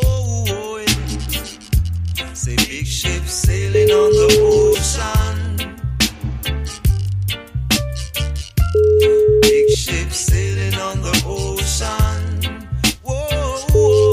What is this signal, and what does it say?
Ringback tone (Europe, 425 Hz).